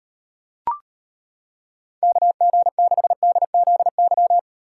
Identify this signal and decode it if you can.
Morse code, heard as a keyed tone, about 31 wpm, 700 Hz. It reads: KG6DZY